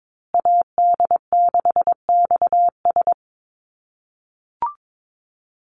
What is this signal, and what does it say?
Morse code, heard as a keyed tone, about 22 wpm, 700 Hz. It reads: AD6XS
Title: USA Call Signs Repeated 2x 22wpm